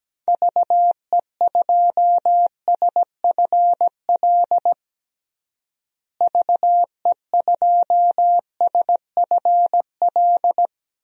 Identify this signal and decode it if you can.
Morse code, heard as a keyed tone, about 17 wpm, 700 Hz. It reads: VE2SFL VE2SFL
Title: DXCC-100 Call Signs - 6 Characters - Repeated 3x - 17wpm